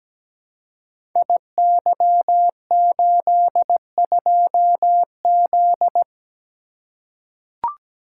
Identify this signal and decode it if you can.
Morse code, heard as a keyed tone, about 17 wpm, 700 Hz. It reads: IY82Z
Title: Code Groups - 5 Letter:Numbers - 17wpm